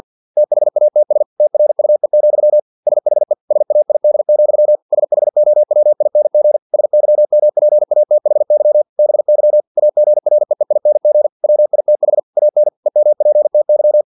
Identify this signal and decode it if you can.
Morse code, heard as a keyed tone, about 50 wpm, 600 Hz. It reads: THATS TRUE, SHE SAID, SHOWING SYMPATHY BY AGREEING WITH AN EMPTY
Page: Never 49wpm